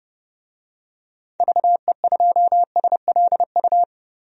Morse code, heard as a keyed tone, about 31 wpm, 700 Hz. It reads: VE2SLU
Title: DXCC-100 Call Signs - 6 Characters - 30wpm